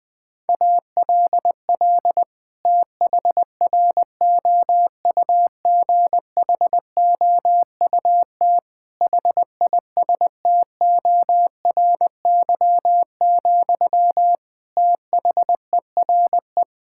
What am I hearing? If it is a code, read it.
Morse code, heard as a keyed tone, about 20 wpm, 700 Hz. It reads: ALL THROUGHOUT HISTORY, THERE
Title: Sentences from Top 1000 Words - Review Part 5 - 20wpm